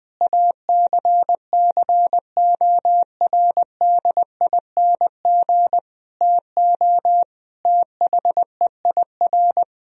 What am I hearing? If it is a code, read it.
Morse code, heard as a keyed tone, about 21 wpm, 700 Hz. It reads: ACCORDING TO THEIR
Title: Sentences from Top 1000 Words - Review Part 4 - 20wpm